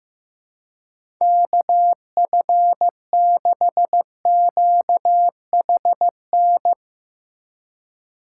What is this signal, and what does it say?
Morse code, heard as a keyed tone, about 15 wpm, 700 Hz. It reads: KF6QHN
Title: USA Call Signs - 8x Character Spacing - 15wpm